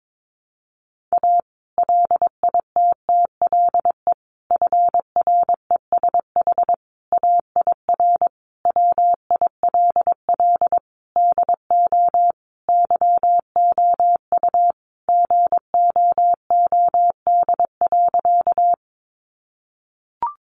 Morse code, heard as a keyed tone, about 22 wpm, 700 Hz. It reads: A LITTLE FRESH AIR WILL DO YOU GOOD.
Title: Sentences from Top 1200 Words 22wpm